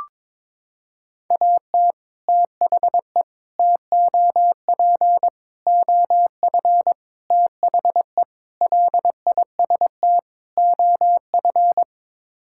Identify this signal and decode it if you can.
Morse code, heard as a keyed tone, about 22 wpm, 700 Hz. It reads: AT THE TOP OF THE LIST OF
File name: practice / 22wpm / Sets of 7 Words - 22wpm